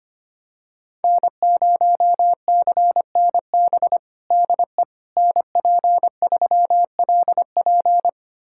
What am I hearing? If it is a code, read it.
Morse code, heard as a keyed tone, about 25 wpm, 700 Hz. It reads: N0CNB DE NP3LP
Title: USA Callsign DE Callsign - 5 Characters - 25wpm